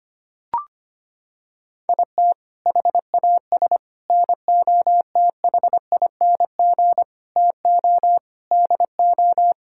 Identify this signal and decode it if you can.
Morse code, heard as a keyed tone, about 25 wpm, 700 Hz. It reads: IT HAS NOTHING TO DO
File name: practice / 25wpm / Sentences with 4-Word Phrases - 25wpm